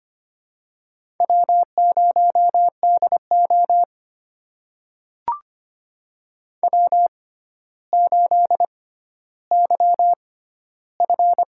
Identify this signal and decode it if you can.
Morse code, heard as a keyed tone, about 25 wpm, spaced as for 13 wpm, 700 Hz. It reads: W0DO W8YF
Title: USA Call Signs - 4 Characters - 6x Character Spacing - 25wpm